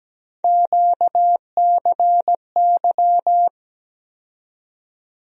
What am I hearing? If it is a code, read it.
Morse code, heard as a keyed tone, about 17 wpm, 700 Hz. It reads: QCY
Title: USA Call Sign Postfixes - 3 Letters 17wpm